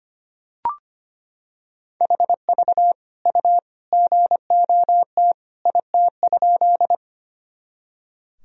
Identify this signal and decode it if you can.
Morse code, heard as a keyed tone, about 25 wpm, 700 Hz. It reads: HV U GOT IT?
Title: Sentences from Top 300 Words and Abbreviations 25wpm